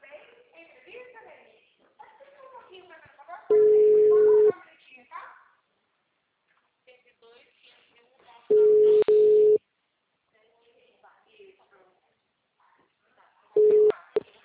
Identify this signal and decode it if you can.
Ringback tone (Europe, 425 Hz), with speech.